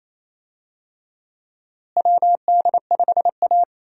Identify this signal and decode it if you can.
Morse code, heard as a keyed tone, about 27 wpm, 700 Hz. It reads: WD5A
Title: USA Call Signs - 4 Characters - Repeated 3x - 28wpm